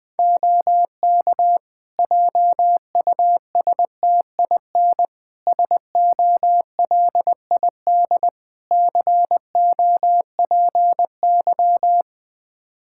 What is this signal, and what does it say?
Morse code, heard as a keyed tone, about 21 wpm, 700 Hz. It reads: OK JUSTIN SOLID COPY
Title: Practice QSO - Exchange 3 - More Info - 20wpm